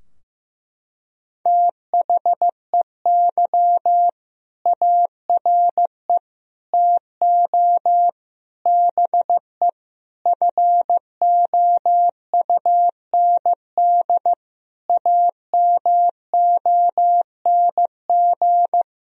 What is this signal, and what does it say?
Morse code, heard as a keyed tone, about 15 wpm, 700 Hz. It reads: THEY ARE TO BE FOUND AMONG
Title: Sentences from Top 1000 Words - Review Part 3 - 15wpm